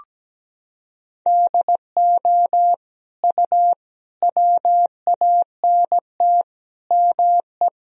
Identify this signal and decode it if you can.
Morse code, heard as a keyed tone, about 17 wpm, 700 Hz. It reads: DO U WANT ME
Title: Sentences from Top 200 Words and Abbreviations 17wpm